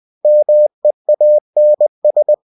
Morse code, heard as a keyed tone, about 20 wpm, 600 Hz. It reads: MEANS
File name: cw-books / winnie-the-pooh / winnie-20wpm-page-0002